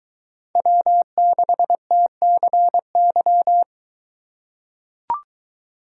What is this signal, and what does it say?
Morse code, heard as a keyed tone, about 21 wpm, 700 Hz. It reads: W6TCY